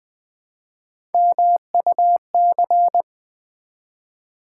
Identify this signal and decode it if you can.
Morse code, heard as a keyed tone, about 20 wpm, 700 Hz. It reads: MUC
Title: Build A Word - 1 Letter At A Time - 4-Letter Words - 20wpm